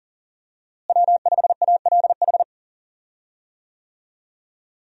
Morse code, heard as a keyed tone, about 40 wpm, 700 Hz. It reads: W5ALH
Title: USA Call Signs - 5 Characters - Repeated 3x - 40wpm